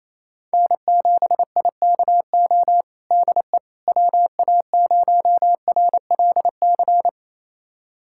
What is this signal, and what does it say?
Morse code, heard as a keyed tone, about 28 wpm, 700 Hz. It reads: N7IKO DE WA0RLC
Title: USA Callsign DE Callsign 28wpm